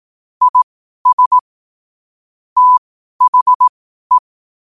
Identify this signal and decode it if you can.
Morse code, heard as a keyed tone, about 18 wpm, spaced as for 12 wpm, 1000 Hz. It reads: IS THE
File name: qotdfiles / 12wpm